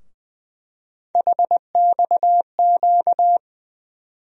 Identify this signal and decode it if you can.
Morse code, heard as a keyed tone, about 20 wpm, 700 Hz. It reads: HXQ
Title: USA Call Sign Postfixes - 3 Letters 20wpm